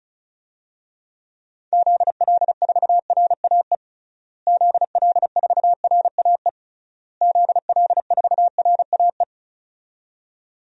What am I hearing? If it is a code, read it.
Morse code, heard as a keyed tone, about 36 wpm, 700 Hz. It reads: ZL4RAE ZL4RAE ZL4RAE
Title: DXCC-100 Call Signs - 6 Characters - Repeated 3x - 35wpm